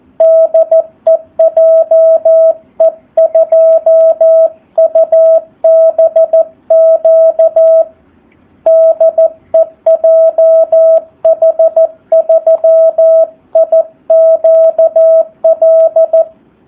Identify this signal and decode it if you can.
Morse code, heard as a keyed tone, about 14 wpm, 650 Hz. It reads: DE JE2UBQ DE JH3IQL
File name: jh8cbh / 2019cw / SP04- w